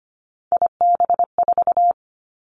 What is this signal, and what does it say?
Morse code, heard as a keyed tone, about 25 wpm, 700 Hz. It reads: IB4
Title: Three Challenging Letter-Number Combinations 25wpm